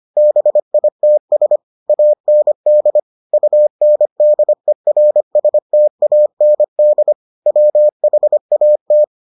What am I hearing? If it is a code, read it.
Morse code, heard as a keyed tone, about 25 wpm, 600 Hz. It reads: BITS AND UNDERSTAND WHAT